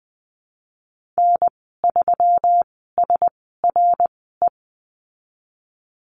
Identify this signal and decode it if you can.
Morse code, heard as a keyed tone, about 20 wpm, spaced as for 15 wpm, 700 Hz. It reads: N3SRE
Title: 122 Lesson 20g - Number 3 - Call Signs 20wpm